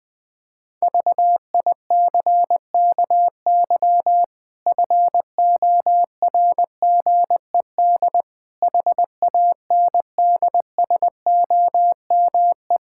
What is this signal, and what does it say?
Morse code, heard as a keyed tone, about 20 wpm, 700 Hz. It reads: VICKY FORGED HANDSOME
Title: Pangram Challenge v2 - 20wpm